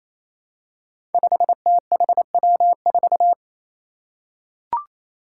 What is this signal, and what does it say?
Morse code, heard as a keyed tone, about 28 wpm, 700 Hz. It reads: 5THW4
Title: Five Challenging Letter-Number Combinations 28wpm